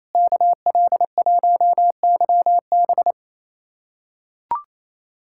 Morse code, heard as a keyed tone, about 28 wpm, 700 Hz. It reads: KL1YB
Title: USA Callsign DE Callsign - 5 Characters - 28wpm